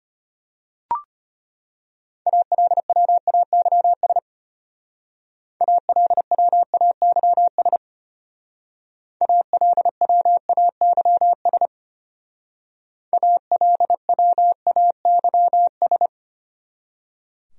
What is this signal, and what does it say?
Morse code, heard as a keyed tone, about 32 wpm, 700 Hz. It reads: ALWAYS ALWAYS ALWAYS ALWAYS